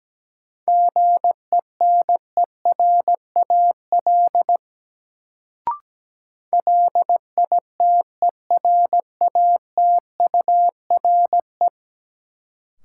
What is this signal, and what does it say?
Morse code, heard as a keyed tone, about 17 wpm, 700 Hz. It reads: GENERAL LITERATURE
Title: Sentences from Top 600 Words - New Word Drill - 17wpm